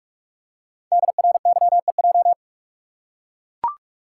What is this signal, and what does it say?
Morse code, heard as a keyed tone, about 44 wpm, 700 Hz. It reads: DRYEJ
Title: Code Groups - 5 Letters - 45wpm